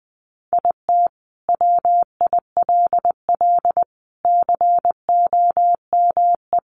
Morse code, heard as a keyed tone, about 20 wpm, 700 Hz. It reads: IT WILL COME
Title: Sentences from Top 1400 Words 20wpm